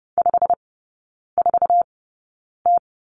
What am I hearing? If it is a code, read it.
Morse code, heard as a keyed tone, about 30 wpm, spaced as for 9 wpm, 700 Hz. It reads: 54T
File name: course / 30:10wpm / 087 Lesson 15e - Number 5 - Three Characters 30wpm